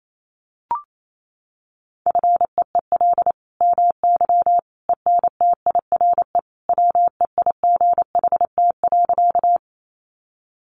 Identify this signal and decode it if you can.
Morse code, heard as a keyed tone, about 28 wpm, 700 Hz. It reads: FEEL MY ENTIRE WEIGHT.
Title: Sentences from Top 800 Words 28wpm